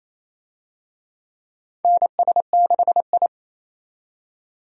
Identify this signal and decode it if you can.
Morse code, heard as a keyed tone, about 27 wpm, 700 Hz. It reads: NS6I
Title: USA Call Signs - 4 Characters - 28wpm